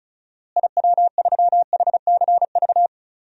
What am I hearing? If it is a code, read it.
Morse code, heard as a keyed tone, about 36 wpm, 700 Hz. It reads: IW3HCV
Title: DXCC-100 Call Signs - 6 Characters - Repeated 3x - 35wpm